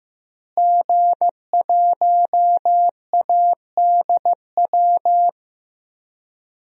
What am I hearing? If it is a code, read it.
Morse code, heard as a keyed tone, about 15 wpm, 700 Hz. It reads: G1ADW
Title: DXCC-100 Call Signs - 5 Characters - Repeated 2x - 15wpm